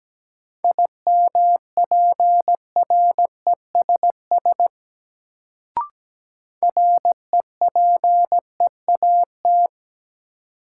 Morse code, heard as a keyed tone, about 17 wpm, 700 Hz. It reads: IMPRESS REPEAT
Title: Sentences from Top 1600 Words - New Word Drill - 17wpm